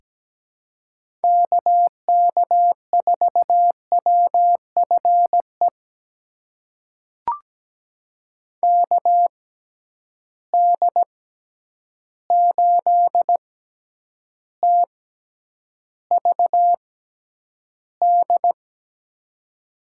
Morse code, heard as a keyed tone, about 17 wpm, spaced as for 9 wpm, 700 Hz. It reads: KK4WFE KD8TVD
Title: USA Call Signs - 6 Characters - 6x Character Spacing - 17wpm